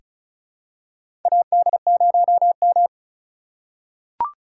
Morse code, heard as a keyed tone, about 34 wpm, 700 Hz. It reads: AD0M
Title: USA Call Signs - 4 Characters - 8x Character Spacing - 35wpm